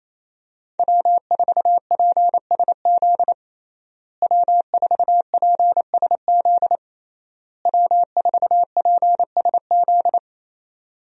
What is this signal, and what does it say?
Morse code, heard as a keyed tone, about 28 wpm, 700 Hz. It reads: W4PSZ W4PSZ W4PSZ
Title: USA Call Signs - 5 Characters - Repeated 3x - 28wpm